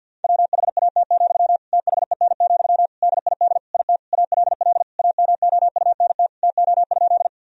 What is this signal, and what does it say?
Morse code, heard as a keyed tone, about 46 wpm, 700 Hz. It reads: WHAT, THEN, DID IT ALL AMOUNT TO?
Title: Sentences from Top 1000 Words - Review Part 4 - 50wpm